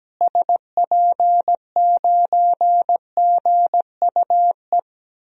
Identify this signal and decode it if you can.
Morse code, heard as a keyed tone, about 17 wpm, 700 Hz. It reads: SP9GUE
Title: DXCC-100 Call Signs - 6 Characters - Repeated 3x - 17wpm